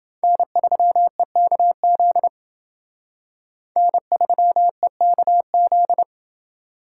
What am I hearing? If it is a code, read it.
Morse code, heard as a keyed tone, about 29 wpm, 700 Hz. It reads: N3EKZ N3EKZ